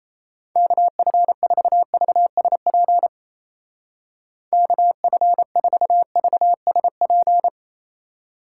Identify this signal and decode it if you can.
Morse code, heard as a keyed tone, about 36 wpm, 700 Hz. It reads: KF4VSP KF4VSP